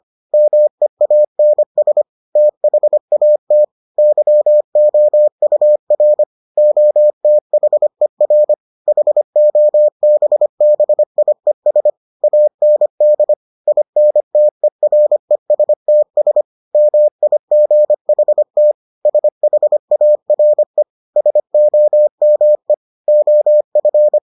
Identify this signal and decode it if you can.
Morse code, heard as a keyed tone, about 25 wpm, 600 Hz. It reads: MEANS THAT YOUR OTHER HOBBIES AND INTERESTS MIGHT SHARE SOME OF